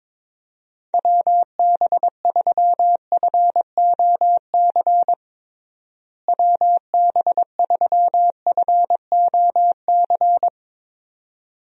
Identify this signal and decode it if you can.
Morse code, heard as a keyed tone, about 22 wpm, 700 Hz. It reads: WB3FOC WB3FOC
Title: USA Call Signs - 6 Characters - Repeated 2x - 22wpm